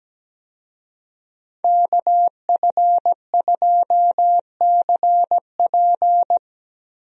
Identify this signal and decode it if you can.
Morse code, heard as a keyed tone, about 17 wpm, 700 Hz. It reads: KF2CP
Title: USA Call Signs - 5 Characters - Repeated 3x - 17wpm